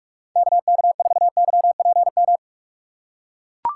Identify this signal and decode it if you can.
Morse code, heard as a keyed tone, about 43 wpm, 700 Hz. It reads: KK4YPM
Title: USA Call Signs - 6 Characters - 3x Character Spacing - 45wpm